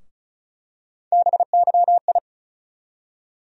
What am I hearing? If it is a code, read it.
Morse code, heard as a keyed tone, about 36 wpm, 700 Hz. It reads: BYI
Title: USA Call Sign Postfixes - 3 Letters 35wpm